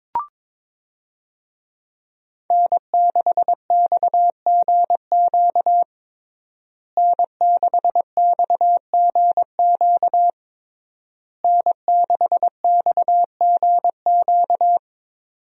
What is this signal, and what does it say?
Morse code, heard as a keyed tone, about 22 wpm, 700 Hz. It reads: N6XGQ N6XGQ N6XGQ
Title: USA Call Signs - 5 Characters - Repeated 3x - 22wpm